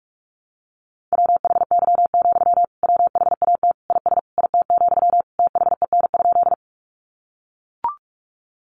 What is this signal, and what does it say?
Morse code, heard as a keyed tone, about 44 wpm, 700 Hz. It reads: WHY, WHAT IS IT, THEN?
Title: Sentences from Top 200 Words 45wpm